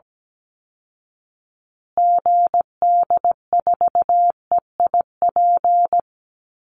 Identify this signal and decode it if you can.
Morse code, heard as a keyed tone, about 17 wpm, 700 Hz. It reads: GD4EIP
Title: Call Signs Repeated 3x - 17wpm